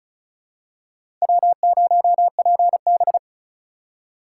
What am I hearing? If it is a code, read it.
Morse code, heard as a keyed tone, about 36 wpm, 700 Hz. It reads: W0PB
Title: USA Call Signs - 4 Characters - 2x Character Spacing - 35wpm